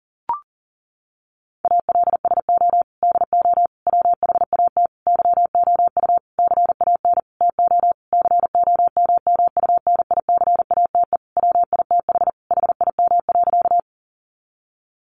Morse code, heard as a keyed tone, about 40 wpm, 700 Hz. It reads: ALSO DO WHAT YOU CAN TO COMMUNICATE WITH HIM.
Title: Sentences from Top 1300 Words 40wpm